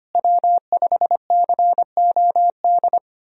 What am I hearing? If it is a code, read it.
Morse code, heard as a keyed tone, about 25 wpm, 700 Hz. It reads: W5COD
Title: USA Call Signs - 5 Characters - Repeated 3x - 25wpm